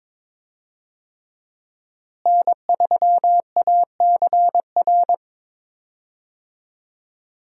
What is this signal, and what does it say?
Morse code, heard as a keyed tone, about 22 wpm, 700 Hz. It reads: N3ACR
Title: USA Call Signs - 5 Characters - Repeated 3x - 22wpm